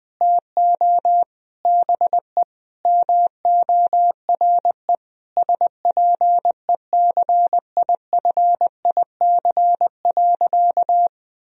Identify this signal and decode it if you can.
Morse code, heard as a keyed tone, about 20 wpm, 700 Hz. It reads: TO BE MORE SPECIFIC.
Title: Sentences from Top 1000 Words - Review Part 3 - 20wpm